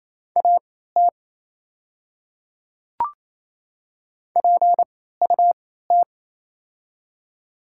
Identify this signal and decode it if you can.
Morse code, heard as a keyed tone, about 27 wpm, spaced as for 16 wpm, 700 Hz. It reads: AT PUT
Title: 3x Character Spacing - Top 300 Words - 28wpm